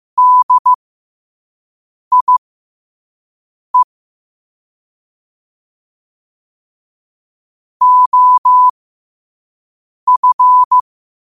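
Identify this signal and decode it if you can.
Morse code, heard as a keyed tone, about 15 wpm, spaced as for 5 wpm, 1000 Hz. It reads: DIE OF
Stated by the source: Morse code practice at 5 wpm.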